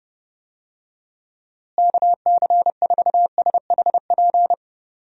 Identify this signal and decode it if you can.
Morse code, heard as a keyed tone, about 30 wpm, 700 Hz. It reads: KC4SHP
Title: USA Call Signs - 6 Characters - Repeated 3x - 30wpm